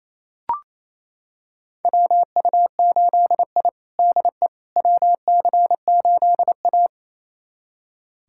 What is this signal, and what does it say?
Morse code, heard as a keyed tone, about 28 wpm, 700 Hz. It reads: WU8I DE WC8A
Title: USA Callsign DE Callsign - 4 Characters - 28wpm